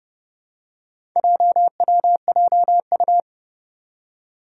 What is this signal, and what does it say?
Morse code, heard as a keyed tone, about 32 wpm, 700 Hz. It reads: JWJU
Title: Four Challenging Letter-Number Combinations 30wpm